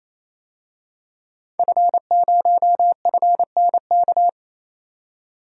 Morse code, heard as a keyed tone, about 28 wpm, 700 Hz. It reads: F0FNK
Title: DXCC-100 Call Signs - 5 Characters - 28wpm